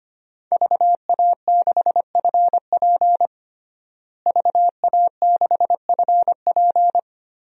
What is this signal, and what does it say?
Morse code, heard as a keyed tone, about 25 wpm, 700 Hz. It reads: VA6FP VA6FP
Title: DXCC-100 Call Signs - 5 Characters - Repeated 2x - 25wpm